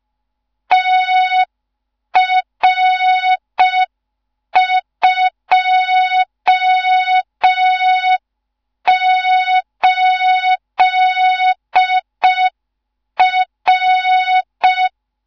Morse code, heard as a keyed tone, about 5 wpm, 3700 Hz. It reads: TR28R
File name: Morze-HA2MN / hang / 15 GYAKORLAT